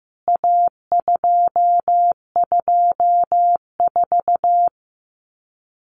Morse code, heard as a keyed tone, about 15 wpm, 700 Hz. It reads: A224
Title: CWT Exchange 15wpm